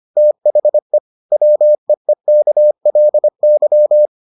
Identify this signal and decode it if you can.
Morse code, heard as a keyed tone, about 25 wpm, 600 Hz. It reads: THE WEEKLY